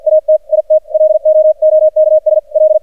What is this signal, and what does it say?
Morse code, heard as a keyed tone, about 51 wpm, spaced as for 45 wpm, 620 Hz. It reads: AT IT LOOKS L
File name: archived / CRSnet